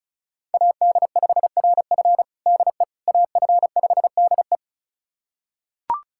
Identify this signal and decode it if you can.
Morse code, heard as a keyed tone, about 34 wpm, 700 Hz. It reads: AD5RF DE AF5DE
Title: USA Callsign DE Callsign - 5 Characters - 35wpm